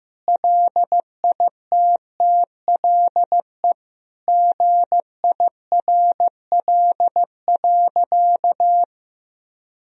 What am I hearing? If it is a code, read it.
Morse code, heard as a keyed tone, about 15 wpm, 700 Hz. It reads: LITTLE GIRL.
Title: Sentences from Top 400 Words and Abbreviations 15wpm